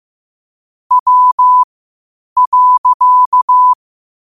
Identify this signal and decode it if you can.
Morse code, heard as a keyed tone, about 15 wpm, spaced as for 8 wpm, 1000 Hz. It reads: W.
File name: qotdfiles / 8wpm